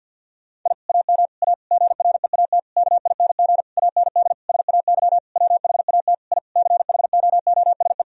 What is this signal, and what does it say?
Morse code, heard as a keyed tone, about 50 wpm, 700 Hz. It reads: I AM A GREAT KING AND SAY WHAT I CHOOSE
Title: Sentences from Top 1000 Words - Review Part 4 - 50wpm